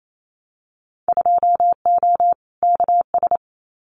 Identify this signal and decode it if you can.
Morse code, heard as a keyed tone, about 28 wpm, 700 Hz. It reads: 2O KS
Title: Winter Field Day Exchanges Encore 28wpm